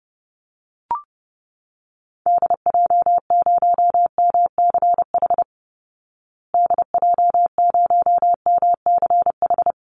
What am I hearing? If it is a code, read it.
Morse code, heard as a keyed tone, about 30 wpm, 700 Hz. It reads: DJ0MCH DJ0MCH
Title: Call Signs Repeated 2x - 30wpm